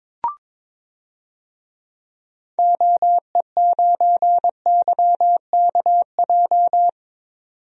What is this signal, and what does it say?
Morse code, heard as a keyed tone, about 21 wpm, 700 Hz. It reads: OE9YKJ
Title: DXCC-100 Call Signs - 6 Characters - 22wpm